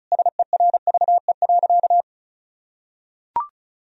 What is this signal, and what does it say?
Morse code, heard as a keyed tone, about 36 wpm, 700 Hz. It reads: SERVE.
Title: Sentences from Top 1000 Words - Review Part 4 - 35wpm